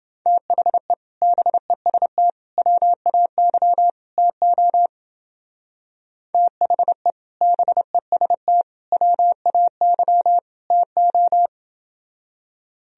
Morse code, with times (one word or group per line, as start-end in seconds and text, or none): 0.3-0.9 THE
1.2-2.3 BEST
2.6-3.9 WAY
4.2-4.9 TO
6.3-7.1 THE
7.4-8.6 BEST
8.9-10.4 WAY
10.7-11.5 TO